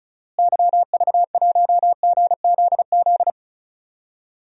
Morse code, heard as a keyed tone, about 34 wpm, 700 Hz. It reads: YV1GZZ
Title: DXCC-100 Call Signs - 6 Characters - Repeated 2x - 35wpm